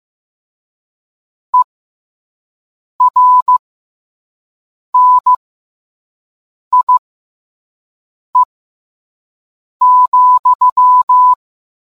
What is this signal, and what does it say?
Morse code, heard as a keyed tone, about 15 wpm, spaced as for 5 wpm, 1000 Hz. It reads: ERNIE,